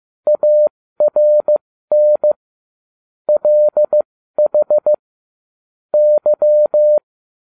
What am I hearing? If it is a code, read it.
Morse code, heard as a keyed tone, about 15 wpm, spaced as for 13 wpm, 600 Hz. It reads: ARN LH Y